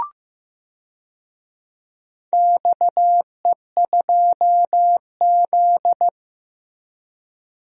Morse code, heard as a keyed tone, about 15 wpm, 700 Hz. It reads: XE2Z
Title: DXCC-100 Call Signs - 4 Characters - Repeated 2x - 15wpm